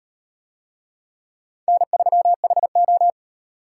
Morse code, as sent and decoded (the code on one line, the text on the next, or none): -. ...-- .... ---
N3HO